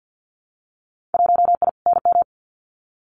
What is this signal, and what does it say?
Morse code, heard as a keyed tone, about 50 wpm, 700 Hz. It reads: 1I NM
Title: Winter Field Day Exchanges Encore 50wpm